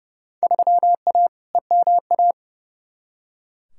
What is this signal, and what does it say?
Morse code, heard as a keyed tone, about 31 wpm, 700 Hz. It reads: 3A EMA
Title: Field Day Exchanges - 30wpm